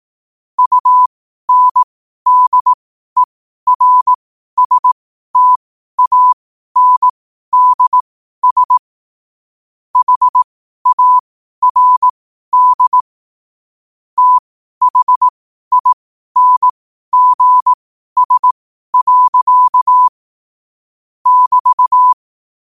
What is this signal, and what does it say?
Morse code, heard as a keyed tone, about 18 wpm, spaced as for 12 wpm, 1000 Hz. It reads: UNDERSTANDS HARD THINGS. <BT>
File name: qotdfiles / 12wpm